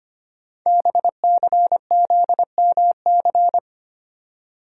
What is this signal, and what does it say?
Morse code, heard as a keyed tone, about 25 wpm, 700 Hz. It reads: BCZMC
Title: Code Groups - 5 Letters - 25wpm